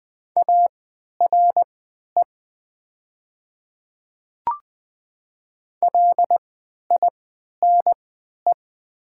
Morse code, heard as a keyed tone, about 20 wpm, spaced as for 11 wpm, 700 Hz. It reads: ARE LINE
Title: 3x Character Spacing - Top 300 Words - 20wpm